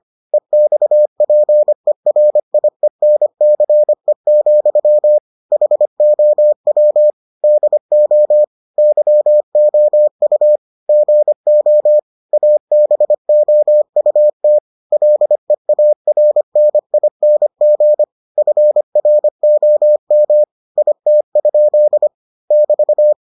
Morse code, heard as a keyed tone, about 25 wpm, 600 Hz. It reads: EXPERIENCE, HOW DO YOU GO ABOUT LEARNING FROM IT? <BT>